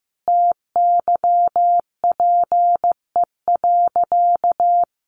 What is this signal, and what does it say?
Morse code, heard as a keyed tone, about 14 wpm, 700 Hz. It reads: TYPE.
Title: Sentences from Top 600 Words 15wpm